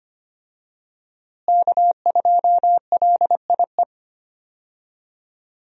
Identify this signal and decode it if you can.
Morse code, heard as a keyed tone, about 25 wpm, 700 Hz. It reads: K2LIE
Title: USA Call Signs - 5 Characters - Repeated 2x - 25wpm